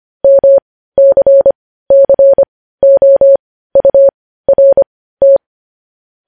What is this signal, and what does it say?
Morse code, heard as a keyed tone, about 25 wpm, spaced as for 15 wpm, 550 Hz. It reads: MCCOURT
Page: News headlines at 15